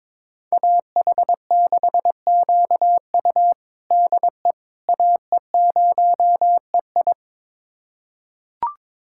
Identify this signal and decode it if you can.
Morse code, heard as a keyed tone, about 22 wpm, 700 Hz. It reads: AH6QU DE AE0EI
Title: USA Callsign DE Callsign - 5 Characters - 22wpm